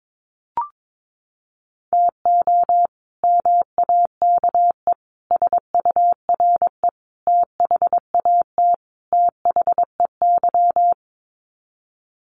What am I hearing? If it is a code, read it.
Morse code, heard as a keyed tone, about 22 wpm, 700 Hz. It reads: TO MAKE SURE THAT THEY
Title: Sets of 5 Words Encore2 22wpm